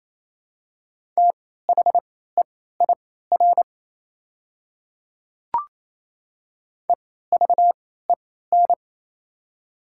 Morse code, heard as a keyed tone, about 27 wpm, spaced as for 16 wpm, 700 Hz. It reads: THEIR EVEN
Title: 3x Character Spacing - Top 200 Words - 28wpm